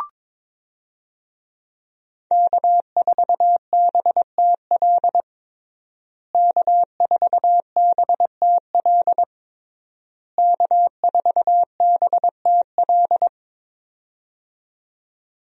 Morse code, heard as a keyed tone, about 22 wpm, 700 Hz. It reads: K4BTL K4BTL K4BTL
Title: USA Call Signs - 5 Characters - Repeated 3x - 22wpm